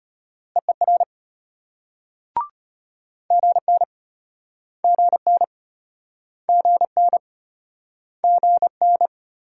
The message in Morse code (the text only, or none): EER GN GN GN GN